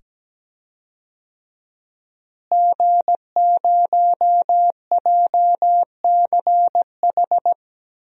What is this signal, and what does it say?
Morse code, heard as a keyed tone, about 17 wpm, 700 Hz. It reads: G0JCH
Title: DXCC-100 Call Signs - 5 Characters - Repeated 2x - 17wpm